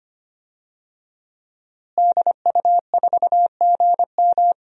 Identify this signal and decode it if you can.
Morse code, heard as a keyed tone, about 25 wpm, 700 Hz. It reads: DU4GM
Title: DXCC-100 Call Signs - 5 Characters - Repeated 3x - 25wpm